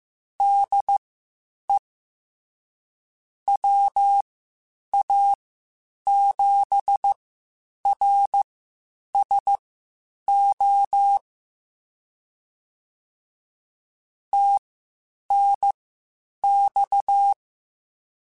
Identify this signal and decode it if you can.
Morse code, heard as a keyed tone, about 15 wpm, spaced as for 8 wpm, 800 Hz. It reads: DE WA7RSO TNX